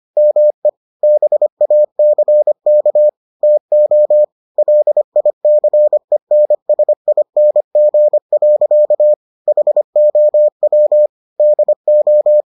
Morse code, heard as a keyed tone, about 25 wpm, 600 Hz. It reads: ME BACK TO LICENSING. HOW DO